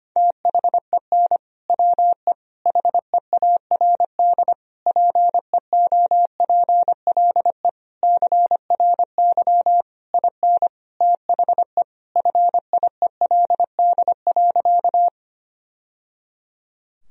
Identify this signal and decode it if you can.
Morse code, heard as a keyed tone, about 25 wpm, 700 Hz. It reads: THEN WE HEARD PEOPLE CRY IN THE FIELD.
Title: Sentences from Top 1000 Words - Review Part 2 - 25wpm